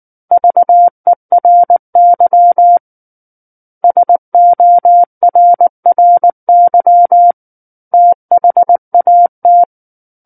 Morse code, heard as a keyed tone, about 19 wpm, 700 Hz. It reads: VERY SORRY THAT
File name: Book One / 10-Chapter 10 - 19 WPM